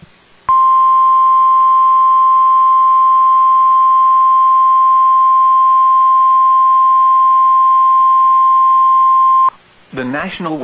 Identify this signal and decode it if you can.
Emergency Alert System (SAME) tones (the 1050 Hz attention tone), with speech. It is heard from 0.4 to 9.5 s.